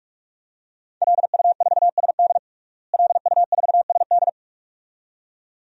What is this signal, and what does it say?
Morse code, heard as a keyed tone, about 46 wpm, 700 Hz. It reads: LU4SD LU4SD
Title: DXCC-100 Call Signs - 5 Characters - Repeated 2x - 45wpm